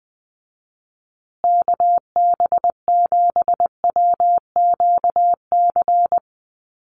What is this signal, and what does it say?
Morse code, heard as a keyed tone, about 20 wpm, 700 Hz. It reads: KB7WQC
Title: USA Call Signs Repeated 2x 20wpm